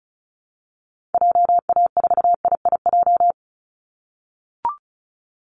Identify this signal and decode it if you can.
Morse code, heard as a keyed tone, about 36 wpm, 700 Hz. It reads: JA4IIJ
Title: Call Signs - 35wpm